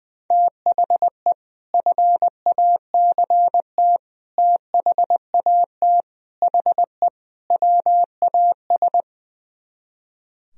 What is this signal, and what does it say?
Morse code, heard as a keyed tone, about 20 wpm, 700 Hz. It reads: THE FACT THAT HE WAS
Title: Speed-Racing - Sets of 5 Words 20wpm